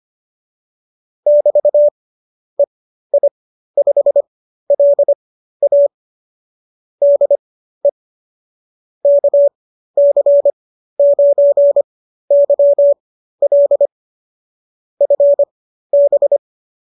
Morse code, heard as a keyed tone, about 25 wpm, spaced as for 13 wpm, 570 Hz. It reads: <BT> EI5LA DE KC9YL FB